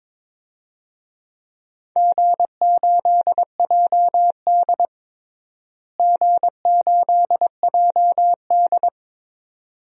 Morse code, heard as a keyed tone, about 22 wpm, 700 Hz. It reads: G8JD G8JD
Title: DXCC-100 Call Signs - 4 Characters - Repeated 2x - 22wpm